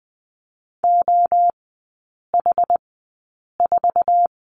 Morse code, heard as a keyed tone, about 20 wpm, spaced as for 8 wpm, 700 Hz. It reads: OH4
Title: 081 Lesson 14e - Number 2 - Three Characters 20wpm